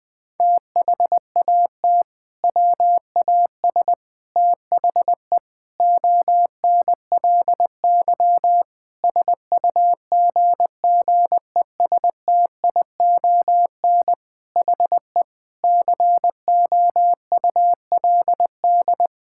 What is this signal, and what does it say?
Morse code, heard as a keyed tone, about 20 wpm, 700 Hz. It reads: THAT WAS THE ONLY SUGGESTION HE COULD
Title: Sentences from Top 1600 Words 20wpm